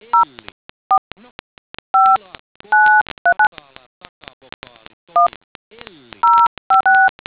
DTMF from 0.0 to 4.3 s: 045939